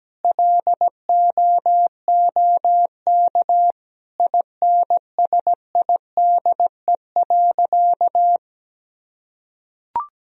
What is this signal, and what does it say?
Morse code, heard as a keyed tone, about 17 wpm, 700 Hz. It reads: LOOK INSIDE.
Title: Sentences from Top 1000 Words - Review Part 2 - 17wpm